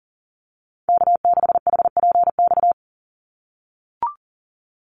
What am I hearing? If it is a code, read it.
Morse code, heard as a keyed tone, about 40 wpm, 700 Hz. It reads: K6HPX
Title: NA SOTA Activator Callsigns 40wpm